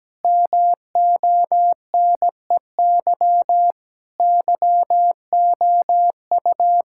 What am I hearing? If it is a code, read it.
Morse code, heard as a keyed tone, about 17 wpm, 700 Hz. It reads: MONEY YOU
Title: Sentences from Top 1000 Words - Review Part 6 - 17wpm